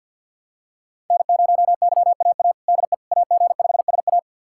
Morse code, heard as a keyed tone, about 52 wpm, 700 Hz. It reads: N0YAA DE AG5SA
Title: USA Callsign DE Callsign 50wpm